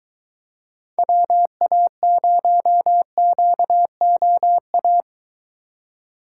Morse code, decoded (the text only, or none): WA0QOA